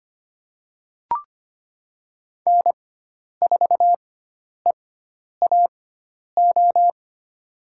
Morse code, heard as a keyed tone, about 25 wpm, spaced as for 10 wpm, 700 Hz. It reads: N4EAO
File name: Lesson 16g - Letter c - Call Signs 25wpm